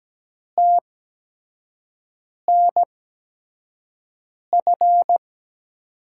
Morse code, heard as a keyed tone, about 17 wpm, spaced as for 5 wpm, 700 Hz. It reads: TNF
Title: USA Call Signs - 8x Character Spacing - 17wpm